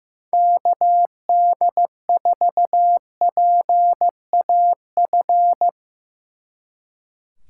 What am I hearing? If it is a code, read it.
Morse code, heard as a keyed tone, about 16 wpm, 700 Hz. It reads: KD4PAF
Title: Speed Racing - USA 2x3 Call Signs 15wpm